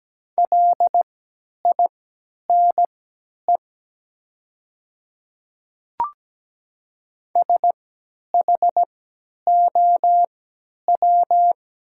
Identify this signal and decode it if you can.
Morse code, heard as a keyed tone, about 17 wpm, spaced as for 10 wpm, 700 Hz. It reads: LINE SHOW
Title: 3x Character Spacing - Top 200 Words - 17wpm